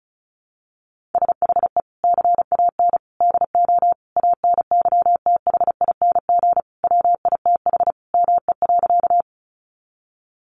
Morse code, heard as a keyed tone, about 34 wpm, 700 Hz. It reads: SHE CAN DO ANYTHING WITH ME.
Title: Sentences from Top 1000 Words 35wpm